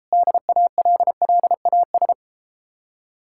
Morse code, heard as a keyed tone, about 33 wpm, 700 Hz. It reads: DALLAS